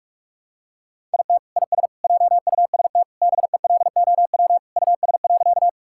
Morse code, heard as a keyed tone, about 46 wpm, 700 Hz. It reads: IT IS JUST BELOW US.
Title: Sentences from Top 1000 Words - Review Part 5 - 45wpm